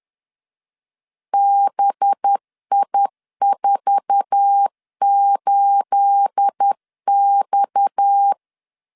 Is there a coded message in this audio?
Yes: Morse code from 1.3 to 8.3 s, heard as a keyed tone, about 10 wpm, 800 Hz. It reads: BI48X